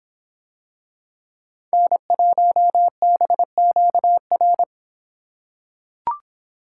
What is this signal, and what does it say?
Morse code, heard as a keyed tone, about 26 wpm, 700 Hz. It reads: N1BQR